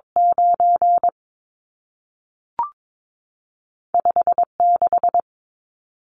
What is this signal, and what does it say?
Morse code, heard as a keyed tone, about 22 wpm, 700 Hz. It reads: 9 5 6
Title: Numbers - 2-Digit 22wpm